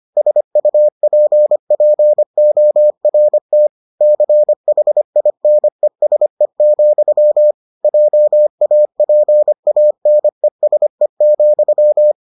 Morse code, heard as a keyed tone, about 25 wpm, 600 Hz. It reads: SUPPORT CHINESE, JAPANESE,